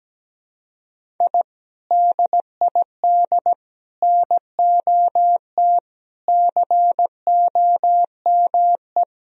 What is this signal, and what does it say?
Morse code, heard as a keyed tone, about 17 wpm, 700 Hz. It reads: I DID NOT COME
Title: Sentences from Top 1000 Words - Review Part 1 - 17wpm